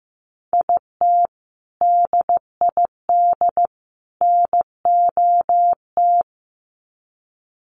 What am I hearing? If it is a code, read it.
Morse code, heard as a keyed tone, about 15 wpm, 700 Hz. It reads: IT DID NOT
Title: Sets of 3 Words 15wpm